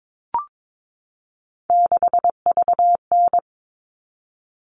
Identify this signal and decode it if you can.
Morse code, heard as a keyed tone, about 22 wpm, 700 Hz. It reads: 6VN
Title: All Cut Numbers - 3-Digit 22wpm